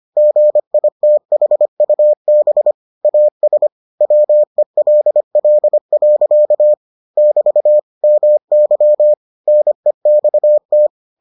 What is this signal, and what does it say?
Morse code, heard as a keyed tone, about 25 wpm, 600 Hz. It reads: GITHUB AS WELL. <BT> MY NEXT